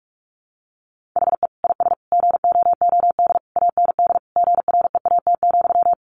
Morse code, heard as a keyed tone, about 44 wpm, 700 Hz. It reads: HE IS GOOD AND GREAT,
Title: Sentences from Top 1400 Words 45wpm